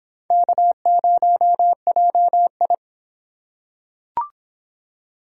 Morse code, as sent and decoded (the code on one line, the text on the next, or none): -.- ----- .--- ..
K0JI